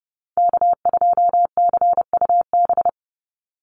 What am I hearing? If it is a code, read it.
Morse code, heard as a keyed tone, about 30 wpm, 700 Hz. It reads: K2CUB
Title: Call Signs Repeated 3x - 30wpm